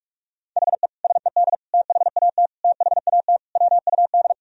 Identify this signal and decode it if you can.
Morse code, heard as a keyed tone, about 41 wpm, 700 Hz. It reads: HE SED THAT THAT WUD
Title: Sentences from Top 200 Words and Abbreviations 45wpm